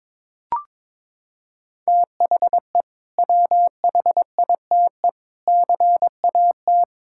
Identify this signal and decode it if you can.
Morse code, heard as a keyed tone, about 22 wpm, 700 Hz. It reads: THE WHITE CAT
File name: practice / 22wpm / Sentences with 2-Word Phrases - 22wpm